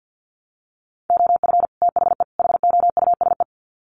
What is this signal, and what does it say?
Morse code, heard as a keyed tone, about 50 wpm, 700 Hz. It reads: OF THE HOUSE
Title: Sets of 3 Words 50wpm